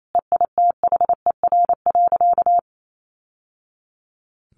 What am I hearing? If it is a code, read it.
Morse code, heard as a keyed tone, about 28 wpm, 700 Hz. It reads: EITHER.
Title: Sentences from Top 1000 Words 28wpm